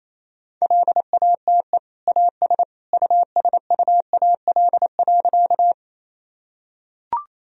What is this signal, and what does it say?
Morse code, heard as a keyed tone, about 28 wpm, 700 Hz. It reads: LATE AS USUAL.
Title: Sentences from Top 1000 Words - Review Part 2 - 28wpm